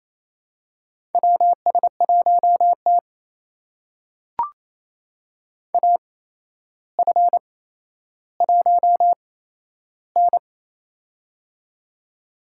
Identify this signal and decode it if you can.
Morse code, heard as a keyed tone, about 28 wpm, spaced as for 12 wpm, 700 Hz. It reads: WS1T AF1N
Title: USA Call Signs - 4 Characters - 8x Character Spacing - 28wpm